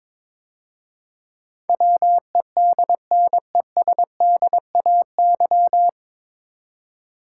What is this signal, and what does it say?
Morse code, heard as a keyed tone, about 22 wpm, 700 Hz. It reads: WEDNESDAY
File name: 9 Letter Words - First 3 Letters Given - 22wpm